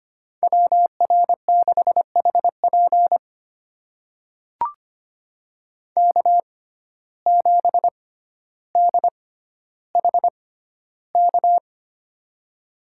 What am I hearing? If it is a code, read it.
Morse code, heard as a keyed tone, about 25 wpm, spaced as for 13 wpm, 700 Hz. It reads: WR6HP K7DHK
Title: USA Call Signs - 5 Characters - 6x Character Spacing - 25wpm